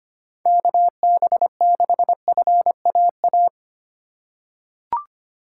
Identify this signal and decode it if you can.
Morse code, heard as a keyed tone, about 25 wpm, 700 Hz. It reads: KB6FAA
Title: USA Call Signs - 2x Character Spacing - 25wpm